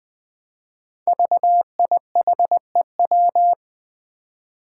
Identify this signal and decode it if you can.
Morse code, heard as a keyed tone, about 20 wpm, 700 Hz. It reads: VIHEW
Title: Code Groups - 5 Letter:Numbers - 20wpm